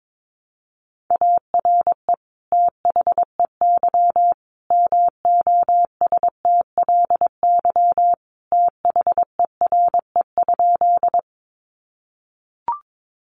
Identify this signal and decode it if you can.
Morse code, heard as a keyed tone, about 22 wpm, 700 Hz. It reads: ARE THEY MOSTLY THERE?
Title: Sentences from Top 1300 Words 22wpm